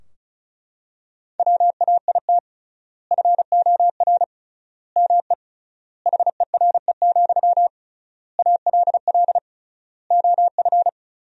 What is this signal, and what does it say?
Morse code, heard as a keyed tone, about 34 wpm, 700 Hz. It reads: WAIT FOR ME HERE, ALL OF
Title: 3x Word-Spacing - Sentences from Top 500 Words - 35wpm